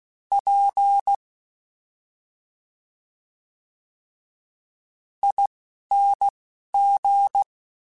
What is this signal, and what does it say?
Morse code, heard as a keyed tone, about 16 wpm, spaced as for 5 wpm, 800 Hz. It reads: PING
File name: Special INGs-mp3s / Special INGs-mp3s / Slow ing @10